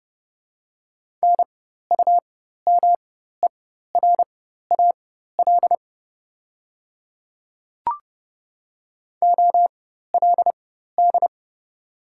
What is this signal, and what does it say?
Morse code, heard as a keyed tone, about 30 wpm, spaced as for 14 wpm, 690 Hz. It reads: NUMERAL OLD
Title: 4x Character Spacing - Top 500 Words - 30wpm